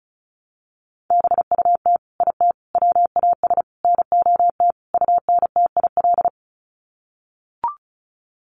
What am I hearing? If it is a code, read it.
Morse code, heard as a keyed tone, about 34 wpm, 700 Hz. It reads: BUT IT WAS NOT UNTIL
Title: Sets of 5 Words Encore2 35wpm